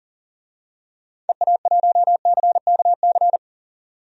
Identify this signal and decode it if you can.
Morse code, heard as a keyed tone, about 40 wpm, 700 Hz. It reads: EA1CKC
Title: DXCC-100 Call Signs - 6 Characters - 40wpm